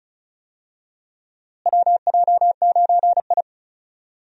Morse code, heard as a keyed tone, about 36 wpm, 700 Hz. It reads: WJ9I
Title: USA Call Signs - 4 Characters - Repeated 3x - 35wpm